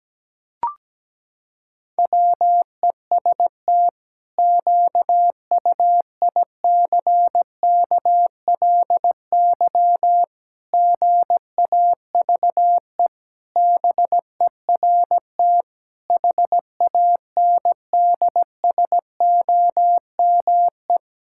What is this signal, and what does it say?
Morse code, heard as a keyed tone, about 17 wpm, 700 Hz. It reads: WEST QUICKLY GAVE BERT HANDSOME
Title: Pangram Challenge v2 - 17wpm